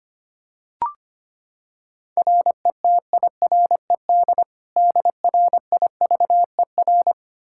Morse code, heard as a keyed tone, about 25 wpm, 700 Hz. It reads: RETIRED DRIVER
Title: Practice QSO - Exchange 3 - More Info - 25wpm